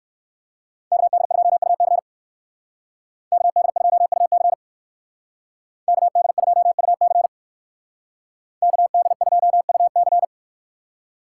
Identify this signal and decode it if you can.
Morse code, heard as a keyed tone, about 57 wpm, 700 Hz. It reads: KD2UC KD2UC KD2UC KD2UC